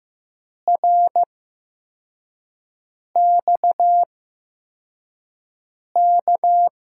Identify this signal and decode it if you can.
Morse code, heard as a keyed tone, about 15 wpm, spaced as for 4 wpm, 700 Hz. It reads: RXK
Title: USA Call Signs - 8x Character Spacing - 15wpm